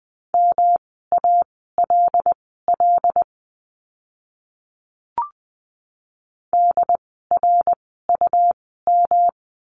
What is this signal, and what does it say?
Morse code, heard as a keyed tone, about 20 wpm, spaced as for 15 wpm, 700 Hz. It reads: MALL DRUM
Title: 108 Lesson 18f - Letter m - Words 20wpm